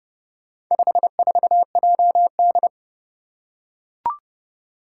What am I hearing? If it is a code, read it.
Morse code, heard as a keyed tone, about 30 wpm, 700 Hz. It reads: 54JD
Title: Four Challenging Letter-Number Combinations 30wpm